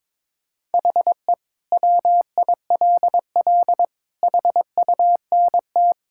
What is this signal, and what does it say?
Morse code, heard as a keyed tone, about 22 wpm, 700 Hz. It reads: HE WILL HUNT
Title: Sentences from Top 1600 Words 22wpm